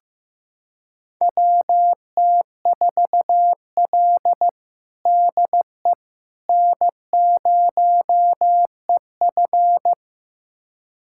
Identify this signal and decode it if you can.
Morse code, heard as a keyed tone, about 15 wpm, 700 Hz. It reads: WT4L DE N0EF
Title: USA Callsign DE Callsign - 4 Characters - 15wpm